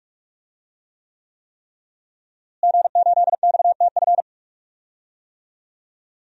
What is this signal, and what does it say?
Morse code, heard as a keyed tone, about 42 wpm, 700 Hz. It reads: G8XTF
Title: DXCC-100 Call Signs - 5 Characters - Repeated 3x - 45wpm